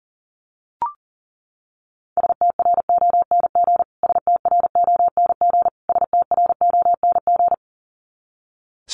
Morse code, heard as a keyed tone, about 40 wpm, 700 Hz. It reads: STRONG STRONG STRONG
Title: Top 500 Words Repeated 40wpm